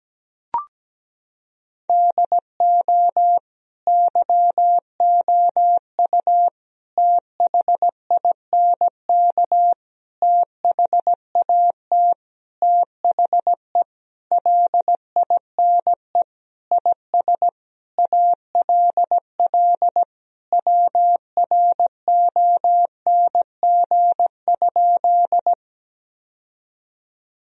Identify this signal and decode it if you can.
Morse code, heard as a keyed tone, about 17 wpm, 700 Hz. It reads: DO YOU THINK THAT THE LINE IS ALL WRONG?
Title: Sentences from Top 1000 Words - Review Part 1 - 17wpm